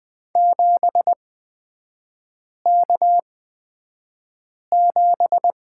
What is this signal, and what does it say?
Morse code, heard as a keyed tone, about 20 wpm, spaced as for 5 wpm, 700 Hz. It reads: 7K7
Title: 224 Lesson 36e - BT - Three Characters 20wpm